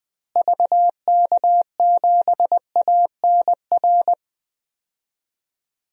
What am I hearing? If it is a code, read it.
Morse code, heard as a keyed tone, about 20 wpm, 700 Hz. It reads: VK7ANR
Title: DXCC-100 Call Signs - 6 Characters - Repeated 2x - 20wpm